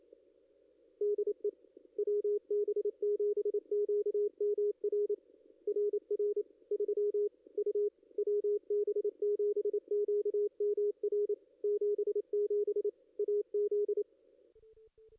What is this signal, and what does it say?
Morse code, heard as a keyed tone, about 27 wpm, 440 Hz, with radio noise. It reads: DE WB7QMR RR 3 U WB7QMR 77 AZ
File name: contests / ss-cw-k5tr-op-n5rz -2300